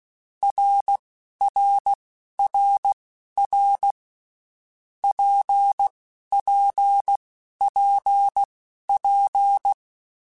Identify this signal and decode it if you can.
Morse code, heard as a keyed tone, about 16 wpm, spaced as for 11 wpm, 800 Hz. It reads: RRRR PPPP
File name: Basic Letter Groups By7Groups / RKPX / RKPX@10